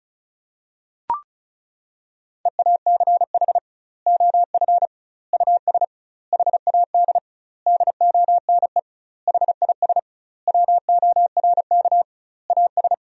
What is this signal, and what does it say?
Morse code, heard as a keyed tone, about 36 wpm, 700 Hz. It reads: EACH OF US HAD DONE HIS WORK AS
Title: 2x Word-Spacing - Sentences from Top 500 Words - 35wpm